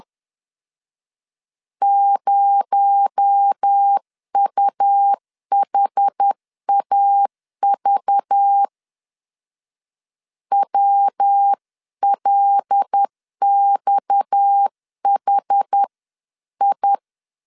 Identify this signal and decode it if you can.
Morse code, heard as a keyed tone, about 10 wpm, 800 Hz. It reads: 0UHAV WLXHI
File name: CW / DET / 1 6 1500C S C 10wpm 4252